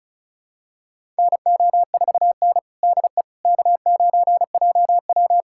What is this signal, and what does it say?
Morse code, heard as a keyed tone, about 36 wpm, 690 Hz. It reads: NO4N DE K9JW
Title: USA Callsign DE Callsign - 4 Characters - 35wpm